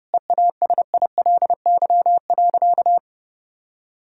Morse code, heard as a keyed tone, about 30 wpm, 700 Hz. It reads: EASILY.
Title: Sentences from Top 1000 Words - Review Part 1 - 30wpm